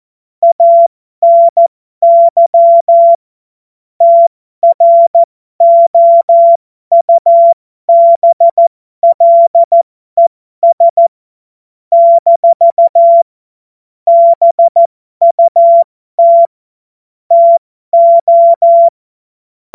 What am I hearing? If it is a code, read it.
Morse code, heard as a keyed tone, about 14 wpm, spaced as for 12 wpm, 680 Hz. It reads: ANY TROUBLES - BUT TO
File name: Morsekurs / Witze